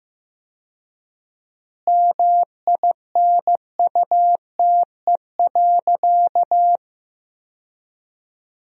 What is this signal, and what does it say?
Morse code, heard as a keyed tone, about 15 wpm, 700 Hz. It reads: MINUTE.
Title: 4x Word-Spacing - Sentences from Top 500 Words - 15wpm